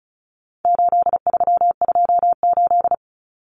Morse code, heard as a keyed tone, about 34 wpm, 700 Hz. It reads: 8328
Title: Numbers - 4-Digit 35wpm